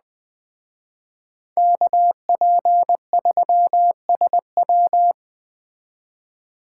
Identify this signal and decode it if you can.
Morse code, heard as a keyed tone, about 20 wpm, 700 Hz. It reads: KP3SW
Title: Speed Racing - USA Call Signs 17wpm